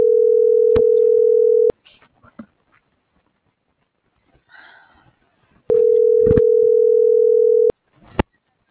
Ringback tone (North America), 440 + 480 Hz.